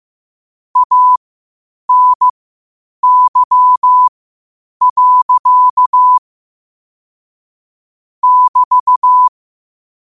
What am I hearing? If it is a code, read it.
Morse code, heard as a keyed tone, about 15 wpm, spaced as for 8 wpm, 1000 Hz. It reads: ANY. <BT>
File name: qotdfiles / 8wpm